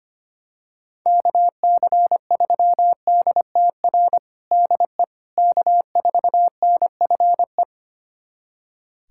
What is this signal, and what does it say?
Morse code, heard as a keyed tone, about 25 wpm, 700 Hz. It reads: KC3DTR DE K4NFE
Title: USA Callsign DE Callsign 25wpm